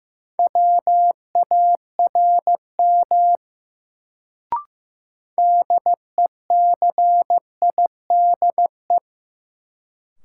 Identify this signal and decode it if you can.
Morse code, heard as a keyed tone, about 15 wpm, 700 Hz. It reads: WARM DECIDE
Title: Sentences from Top 500 Words - New Word Drill - 15wpm